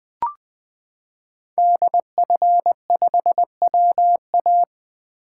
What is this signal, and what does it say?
Morse code, heard as a keyed tone, about 20 wpm, 700 Hz. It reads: DF5WA
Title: 2023 SOTA Chaser Call Signs - 20wpm